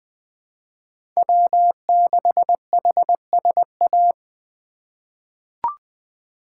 Morse code, heard as a keyed tone, about 20 wpm, 700 Hz. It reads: W6HSA
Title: USA Call Signs - 5 Characters - 2x Character Spacing - 20wpm